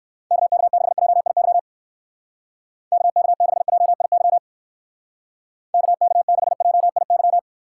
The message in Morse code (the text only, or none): KK6JIY KK6JIY KK6JIY